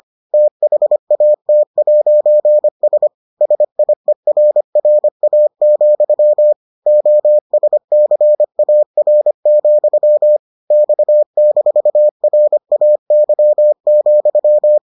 Morse code, heard as a keyed tone, about 25 wpm, 600 Hz. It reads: THAT'S SIERRA, OSCAR, X-RAY,